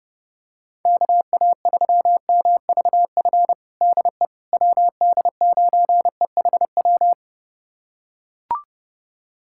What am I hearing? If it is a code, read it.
Morse code, heard as a keyed tone, about 30 wpm, 690 Hz. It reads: KA3MVF DE WD9EHW
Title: USA Callsign DE Callsign 30wpm